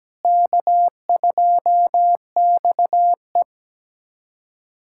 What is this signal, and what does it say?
Morse code, heard as a keyed tone, about 17 wpm, 700 Hz. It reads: K2XE
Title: USA Callsign DE Callsign - 4 Characters - 17wpm